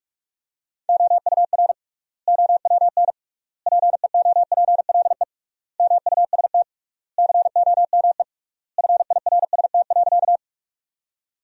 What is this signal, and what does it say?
Morse code, heard as a keyed tone, about 46 wpm, 700 Hz. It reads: OUR OWN PEOPLE MUST COME FIRST.
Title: 3x Word-Spacing - Sentences from Top 300 Words - 45wpm